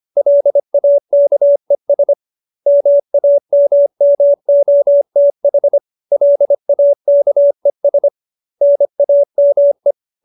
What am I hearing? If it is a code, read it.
Morse code, heard as a keyed tone, about 25 wpm, 570 Hz. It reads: LAKES MAMMOTH LAKES NAME
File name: QSO305 25